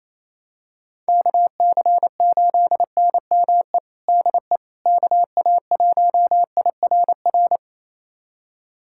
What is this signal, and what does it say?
Morse code, heard as a keyed tone, about 28 wpm, 700 Hz. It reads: KC8NME DE KA1IRR
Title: USA Callsign DE Callsign - 6 Characters - 28wpm